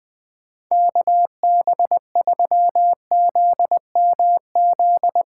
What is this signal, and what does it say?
Morse code, heard as a keyed tone, about 23 wpm, 700 Hz. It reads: KB3ZMZ
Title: Speed Racing - USA 2x3 Call Signs 20wpm